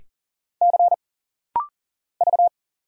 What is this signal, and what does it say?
Morse code, heard as a keyed tone, about 39 wpm, 700 Hz. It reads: C V
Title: Single Letter-Number - Rapid-Fire 40wpm